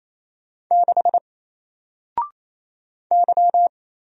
Morse code, heard as a keyed tone, about 28 wpm, 700 Hz. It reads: 6 Y
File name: Single-Letter-Number-28wpm